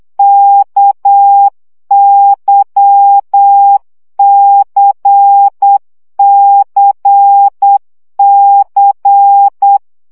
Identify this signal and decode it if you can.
Morse code, heard as a keyed tone, about 8 wpm, 800 Hz. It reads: KYCCC
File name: CW gyakorlas k c y 7 baud